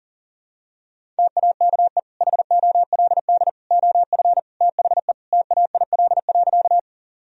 Morse code, heard as a keyed tone, about 40 wpm, 700 Hz. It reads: TAKE HOLD OF THE TAIL.
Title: Sentences from Top 1000 Words - Review Part 3 - 40wpm